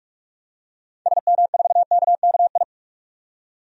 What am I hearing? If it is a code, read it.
Morse code, heard as a keyed tone, about 43 wpm, 700 Hz. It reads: SM4KKI
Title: DXCC-100 Call Signs - 6 Characters - 45wpm